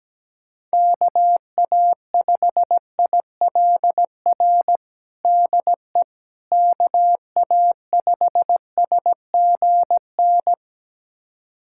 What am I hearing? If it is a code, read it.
Morse code, heard as a keyed tone, about 17 wpm, 700 Hz. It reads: KA5ILR DE KA5SGN
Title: USA Callsign DE Callsign - 6 Characters - 17wpm